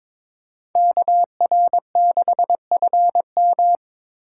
Morse code, heard as a keyed tone, about 21 wpm, 700 Hz. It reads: KR6FM
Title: USA Call Signs - 8x Character Spacing - 22wpm